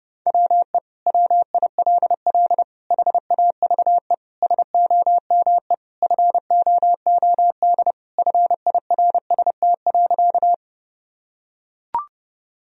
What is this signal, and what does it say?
Morse code, heard as a keyed tone, about 30 wpm, 700 Hz. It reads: WE WILL HAVE SOME FOOD FIRST.
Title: Sentences from Top 1000 Words - Review Part 1 - 30wpm